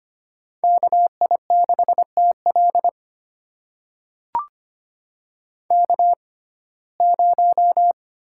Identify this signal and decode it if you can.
Morse code, heard as a keyed tone, about 25 wpm, spaced as for 18 wpm, 700 Hz. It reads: KI6TL K0
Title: USA Call Signs - 6x Character Spacing - 25wpm